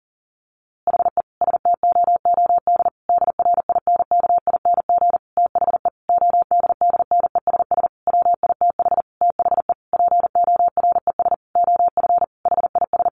Morse code, heard as a keyed tone, about 40 wpm, 700 Hz. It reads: HE STOOD DRINKING THE ODDNESS WITH THE PORES OF HIS
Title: A Sound Of Thunder 40wpm